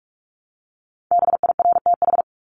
Morse code, heard as a keyed tone, about 44 wpm, 700 Hz. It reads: BIRTH
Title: Top 751-1000 Words 45wpm